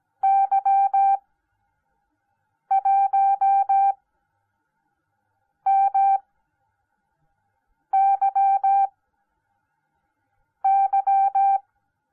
Morse code, heard as a keyed tone, about 17 wpm, spaced as for 5 wpm, 780 Hz. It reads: Y1MYY